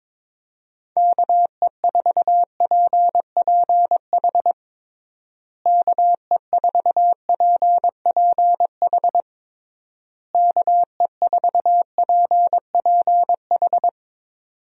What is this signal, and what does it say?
Morse code, heard as a keyed tone, about 22 wpm, 700 Hz. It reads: KE4PPH KE4PPH KE4PPH
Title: USA Call Signs - 6 Characters - Repeated 3x - 22wpm